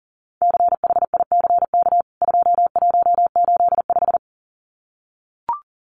Morse code, heard as a keyed tone, about 40 wpm, 700 Hz. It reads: CHICK 2185
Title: CWT Exchange 40wpm